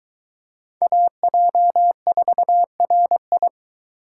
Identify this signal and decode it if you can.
Morse code, heard as a keyed tone, about 22 wpm, 700 Hz. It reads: AJ4RI